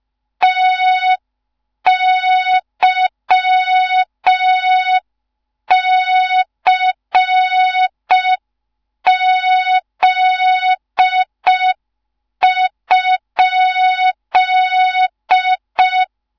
Morse code, heard as a keyed tone, about 5 wpm, 3700 Hz. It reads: TYCZ?
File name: Morze-HA2MN / hang / 15 GYAKORLAT